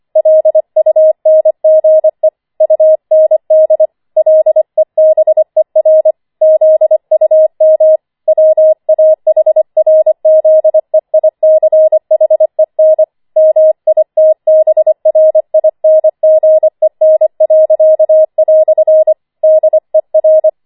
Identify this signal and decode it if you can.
Morse code, heard as a keyed tone, about 25 wpm, 630 Hz. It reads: LUNGE UND LEBER ZUM WAHRZEICHEN MITBRINGEN." DER
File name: 25wpm